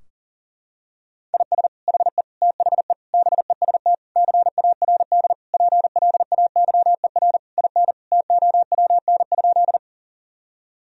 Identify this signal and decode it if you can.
Morse code, heard as a keyed tone, about 40 wpm, 700 Hz. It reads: IS HE THE BEST CARD PLAYER IN TOWN?
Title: Sentences from Top 1000 Words - Review Part 4 - 40wpm